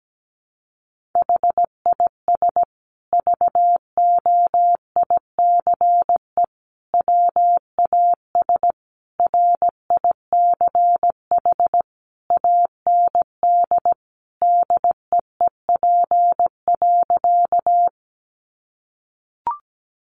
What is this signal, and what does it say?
Morse code, heard as a keyed tone, about 17 wpm, 700 Hz. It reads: HIS VOICE WAS RICH AND DEEP.
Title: Sentences from Top 1100 Words 17wpm